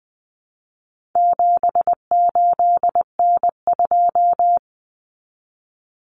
Morse code, heard as a keyed tone, about 20 wpm, 700 Hz. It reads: 78N2
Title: Common Cut Numbers - 4-Digit 20wpm